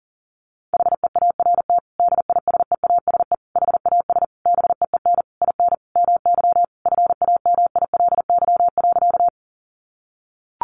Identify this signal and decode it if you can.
Morse code, heard as a keyed tone, about 40 wpm, 700 Hz. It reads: HEART DISEASE HAS BEEN IN MY FAMILY.
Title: Sentences from Top 1200 Words 40wpm